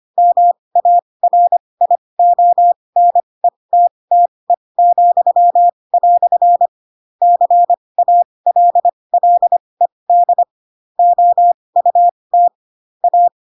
Morse code, heard as a keyed tone, about 25 wpm, spaced as for 20 wpm, 700 Hz. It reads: MARIONETTE," CALLED OUT A